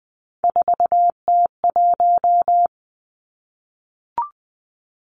Morse code, heard as a keyed tone, about 20 wpm, 700 Hz. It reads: 4T1
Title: Three Challenging Letter-Number Combinations 20wpm